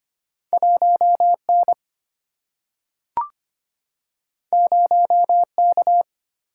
Morse code, heard as a keyed tone, about 25 wpm, 700 Hz. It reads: 1N 0K
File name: Two Letter-Number Combinations - 25wpm